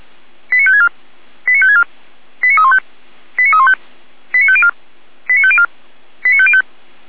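Five-tone selective call (ZVEI), heard as quick runs of five tones, tones of about 80 ms. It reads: 89564 89564 89315 89315 89504 89504 89505
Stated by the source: Five-tone ZVEI selective call 89280 received by SDR.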